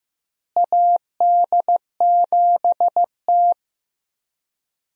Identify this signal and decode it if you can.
Morse code, heard as a keyed tone, about 15 wpm, 700 Hz. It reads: AD7T
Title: USA Call Signs - 4 Characters - Repeated 2x - 15wpm